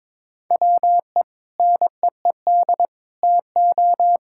Morse code, heard as a keyed tone, about 22 wpm, 700 Hz. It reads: WE NEED TO
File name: Sentences with 7-Word Phrases - 22wpm